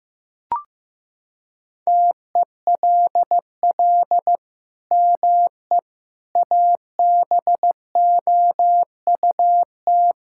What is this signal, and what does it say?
Morse code, heard as a keyed tone, about 14 wpm, 700 Hz. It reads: TELL ME ABOUT
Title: Sentences from Top 1000 Words - Review Part 5 - 15wpm